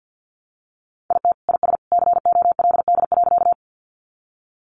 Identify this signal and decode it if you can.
Morse code, heard as a keyed tone, about 50 wpm, 700 Hz. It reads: IT IS COLD.
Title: Sentences from Top 400 Words 50wpm